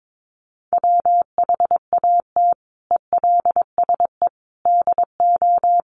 Morse code, heard as a keyed tone, about 22 wpm, 700 Hz. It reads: WHAT ELSE DO
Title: Sentences from Top 1100 Words 22wpm